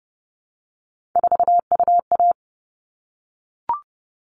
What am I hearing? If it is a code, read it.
Morse code, heard as a keyed tone, about 30 wpm, 700 Hz. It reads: 4UA
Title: All Cut Numbers - 3-Digit 30wpm